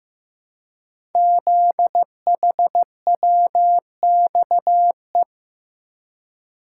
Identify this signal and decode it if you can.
Morse code, heard as a keyed tone, about 15 wpm, 700 Hz. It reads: ZHWXE
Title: Code Groups - 5 Letters - 15wpm